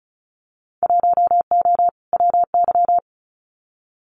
Morse code, heard as a keyed tone, about 36 wpm, 700 Hz. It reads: 1O WY
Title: Winter Field Day Exchanges Encore 35wpm